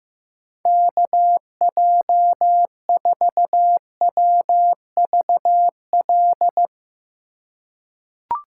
Morse code, heard as a keyed tone, about 16 wpm, 700 Hz. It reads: KJ4WVL
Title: USA Call Signs - 2x Character Spacing - 15wpm